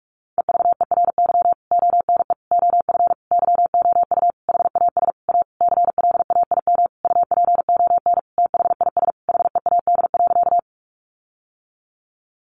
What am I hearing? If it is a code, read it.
Morse code, heard as a keyed tone, about 44 wpm, 700 Hz. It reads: EVERY ONE OF YOU HAS A CLAIM UPON THIS HEAD.
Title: Sentences from Top 1500 Words 45wpm